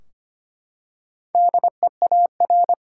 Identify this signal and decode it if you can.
Morse code, heard as a keyed tone, about 25 wpm, 700 Hz. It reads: DEAR
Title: Four Letter Words - 25wpm